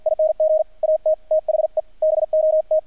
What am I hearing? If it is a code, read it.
Morse code, heard as a keyed tone, about 48 wpm, 630 Hz, with radio noise. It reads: ING AT THE BOT
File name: archived / CRSnet